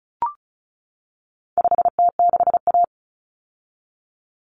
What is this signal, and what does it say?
Morse code, heard as a keyed tone, about 34 wpm, 700 Hz. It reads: 5T6A
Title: Common Cut Numbers - 4-Digit 35wpm